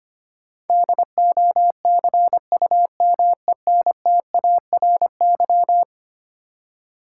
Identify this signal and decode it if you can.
Morse code, heard as a keyed tone, about 25 wpm, 700 Hz. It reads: DOCUMENTARY
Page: morse-code-ninja-practice-25wpm